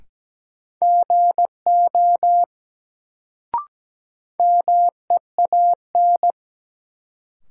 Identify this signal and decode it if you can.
Morse code, heard as a keyed tone, about 17 wpm, 700 Hz. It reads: GO MEAN
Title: Sets of 2 Words - New Word Drill - 17wpm